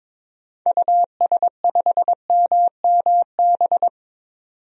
Morse code, heard as a keyed tone, about 22 wpm, 700 Hz. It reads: US5MMB
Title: DXCC-100 Call Signs - 6 Characters - 22wpm